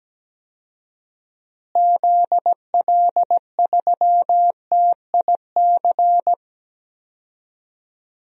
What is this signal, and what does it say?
Morse code, heard as a keyed tone, about 17 wpm, 700 Hz. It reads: ZL3TIC
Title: DXCC-100 Call Signs - 6 Characters - Repeated 2x - 17wpm